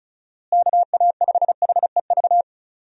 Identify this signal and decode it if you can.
Morse code, heard as a keyed tone, about 34 wpm, 700 Hz. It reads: KA5HEV
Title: USA Call Signs - 6 Characters - Repeated 3x - 35wpm